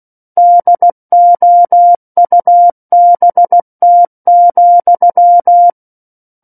Morse code, heard as a keyed tone, about 16 wpm, 700 Hz. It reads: DOUBT,
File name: Book One / 07-Chapter 7 - 16 WPM